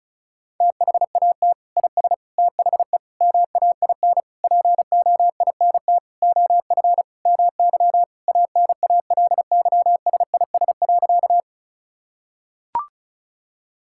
Morse code, heard as a keyed tone, about 36 wpm, 700 Hz. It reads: THAT IS THE MAIN POINT OF MY ANALYSIS.
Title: Sentences from Top 1000 Words - Review Part 5 - 35wpm